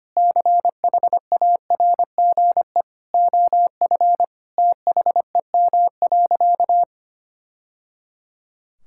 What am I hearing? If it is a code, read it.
Morse code, heard as a keyed tone, about 25 wpm, 700 Hz. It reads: CHARGE OF THEM.
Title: Sentences from Top 1000 Words - Review Part 6 - 25wpm